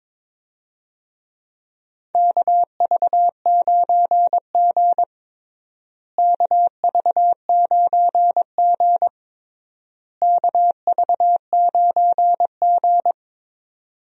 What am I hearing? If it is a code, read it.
Morse code, heard as a keyed tone, about 22 wpm, 700 Hz. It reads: KV9G KV9G KV9G
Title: USA Call Signs - 4 Characters - Repeated 3x - 22wpm